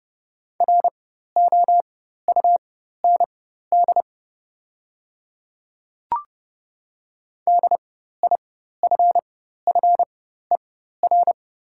Morse code, heard as a keyed tone, about 31 wpm, spaced as for 14 wpm, 700 Hz. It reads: ROUND DIFFER
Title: 4x Character Spacing - Top 300 Words - 30wpm